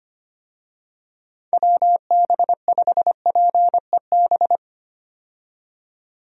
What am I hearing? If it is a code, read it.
Morse code, heard as a keyed tone, about 25 wpm, 700 Hz. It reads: WB5PEB
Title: USA Call Signs - 6 Characters - 25wpm